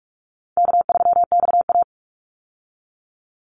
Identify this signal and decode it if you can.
Morse code, heard as a keyed tone, about 44 wpm, 700 Hz. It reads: K3XA
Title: Call Signs - 45wpm